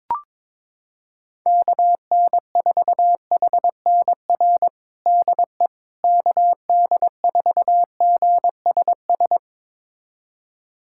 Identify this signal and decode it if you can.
Morse code, heard as a keyed tone, about 22 wpm, 700 Hz. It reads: KN4HNR DE KD4GSS
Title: USA Callsign DE Callsign - 6 Characters - 22wpm